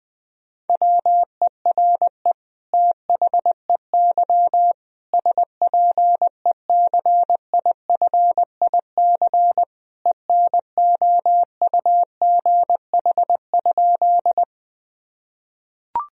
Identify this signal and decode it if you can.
Morse code, heard as a keyed tone, about 21 wpm, 700 Hz. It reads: WERE THEY SPECIFIC ENOUGH?
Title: Sentences from Top 1000 Words - Review Part 3 - 20wpm